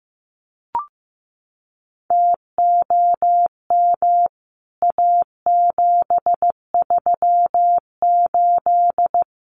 Morse code, heard as a keyed tone, about 15 wpm, 700 Hz. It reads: TOM A738
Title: CWT Exchange 15wpm